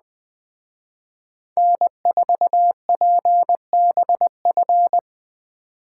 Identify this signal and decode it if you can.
Morse code, heard as a keyed tone, about 20 wpm, 700 Hz. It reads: N4PBF